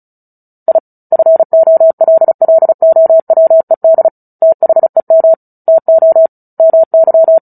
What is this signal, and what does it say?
Morse code, heard as a keyed tone, about 34 wpm, 650 Hz. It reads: I FOLLOWED THEM TO MY
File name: Book Two / 09-Chapter 9 - 35 WPM